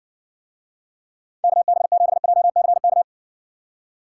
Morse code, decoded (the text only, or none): KB7JCK